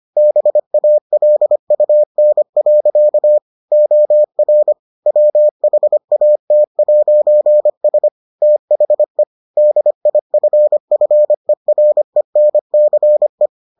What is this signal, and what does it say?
Morse code, heard as a keyed tone, about 25 wpm, 600 Hz. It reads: BALUN. OR WHAT'S THE DIFFERENCE